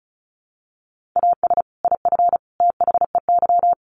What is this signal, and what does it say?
Morse code, heard as a keyed tone, about 34 wpm, 700 Hz. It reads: AS IF THEY
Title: Sets of 4 Words 35wpm